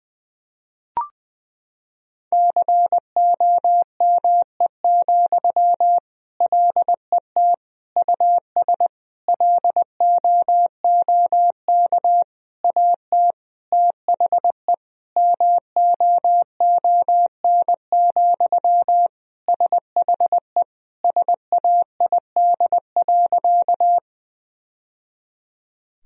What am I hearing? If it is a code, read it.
Morse code, heard as a keyed tone, about 20 wpm, 700 Hz. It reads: COME, LET US LOOK AT THE MOON, SHE SAID.
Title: Sentences from Top 1000 Words - Review Part 3 - 20wpm